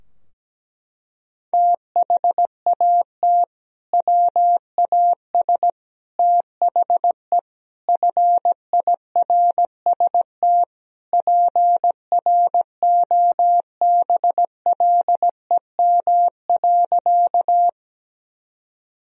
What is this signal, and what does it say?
Morse code, heard as a keyed tone, about 17 wpm, 700 Hz. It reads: THAT WAS THE FIRST PROBLEM.
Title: Sentences from Top 1000 Words - Review Part 2 - 17wpm